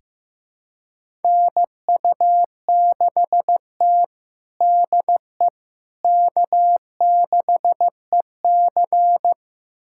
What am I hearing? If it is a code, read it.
Morse code, heard as a keyed tone, about 16 wpm, 700 Hz. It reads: NU6T DE K6EC
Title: USA Callsign DE Callsign - 4 Characters - 15wpm